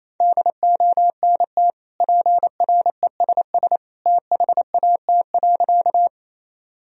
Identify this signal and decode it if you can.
Morse code, heard as a keyed tone, about 28 wpm, 700 Hz. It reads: DONT PRESS THAT.
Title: Sentences from Top 1000 Words - Review Part 1 - 28wpm